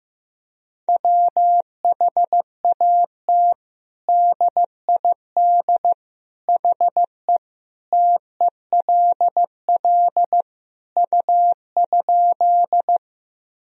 Morse code, heard as a keyed tone, about 15 wpm, 700 Hz. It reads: WHAT DID HE TELL U?
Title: Sentences from Top 200 Words and Abbreviations 15wpm